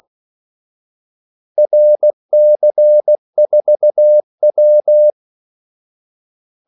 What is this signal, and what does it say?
Morse code, heard as a keyed tone, about 16 wpm, 600 Hz. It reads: RC4W
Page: Kaldesignaler 16 wpm